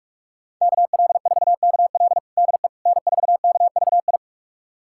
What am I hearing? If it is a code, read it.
Morse code, heard as a keyed tone, about 46 wpm, 700 Hz. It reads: KL4KL DE N4KVI
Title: USA Callsign DE Callsign - 5 Characters - 45wpm